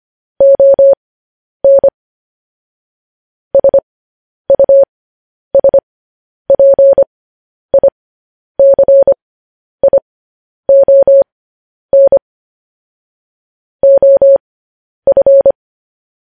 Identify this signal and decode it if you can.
Morse code, heard as a keyed tone, about 25 wpm, spaced as for 10 wpm, 550 Hz. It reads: ON SUSPICION OF
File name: podcast / News headlines-1x-25x10